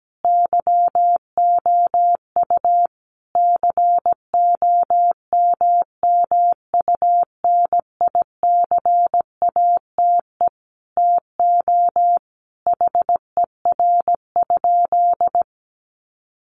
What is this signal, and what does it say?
Morse code, heard as a keyed tone, about 17 wpm, 700 Hz. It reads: YOU COMMUNICATE TO HER?
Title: Sentences from Top 1300 Words 17wpm